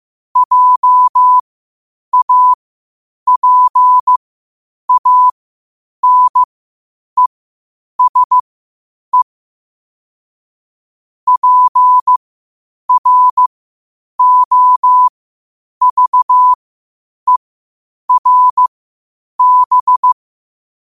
Morse code, heard as a keyed tone, about 15 wpm, spaced as for 8 wpm, 1000 Hz. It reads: JAPANESE PROVERB
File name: qotdfiles / 8wpm